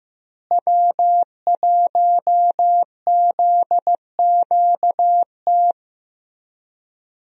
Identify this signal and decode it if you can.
Morse code, heard as a keyed tone, about 15 wpm, 700 Hz. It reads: W1ZQT
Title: USA Call Signs - 5 Characters - Repeated 2x - 15wpm